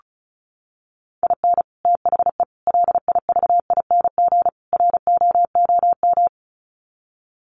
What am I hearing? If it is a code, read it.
Morse code, heard as a keyed tone, about 34 wpm, 700 Hz. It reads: IN THE LIVING ROOM
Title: Sets of 4 Words 35wpm